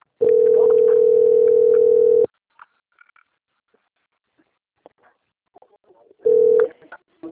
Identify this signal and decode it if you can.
Ringback tone (North America), 440 + 480 Hz.